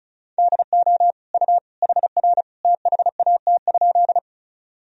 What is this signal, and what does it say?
Morse code, heard as a keyed tone, about 36 wpm, 700 Hz. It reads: DO U HR THAT?
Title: Sentences from Top 300 Words and Abbreviations 35wpm